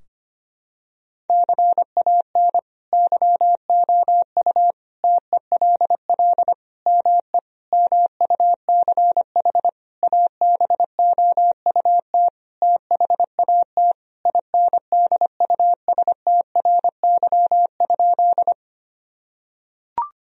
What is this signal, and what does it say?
Morse code, heard as a keyed tone, about 25 wpm, 700 Hz. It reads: CAN YOU TELL ME MUCH ABOUT THAT INDUSTRY?
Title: Sentences from Top 1000 Words - Review Part 4 - 25wpm